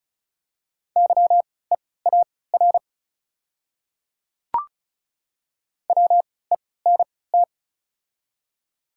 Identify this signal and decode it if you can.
Morse code, heard as a keyed tone, about 34 wpm, spaced as for 20 wpm, 700 Hz. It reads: YEAR WENT
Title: 3x Character Spacing - Top 300 Words - 35wpm